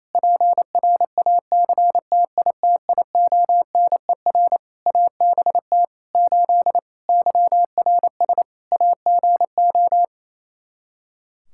Morse code, heard as a keyed tone, about 28 wpm, 700 Hz. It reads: PRACTITIONER ABT 8 YRS AGO
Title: Practice QSO - Exchange 3 - More Info - 28wpm